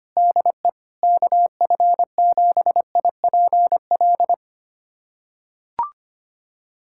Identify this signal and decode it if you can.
Morse code, heard as a keyed tone, about 25 wpm, 700 Hz. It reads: DE KF7IPL
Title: USA Callsign DE Callsign - 6 Characters - 25wpm